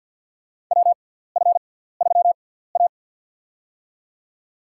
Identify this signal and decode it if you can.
Morse code, heard as a keyed tone, about 52 wpm, spaced as for 17 wpm, 700 Hz. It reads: WF3A
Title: USA Call Signs - 6x Character Spacing - 50wpm